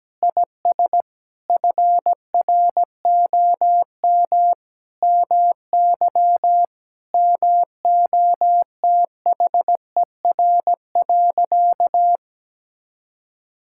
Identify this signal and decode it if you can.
Morse code, heard as a keyed tone, about 17 wpm, 700 Hz. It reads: IS FROM MY MOTHER.
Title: Sentences from Top 1000 Words - Review Part 6 - 17wpm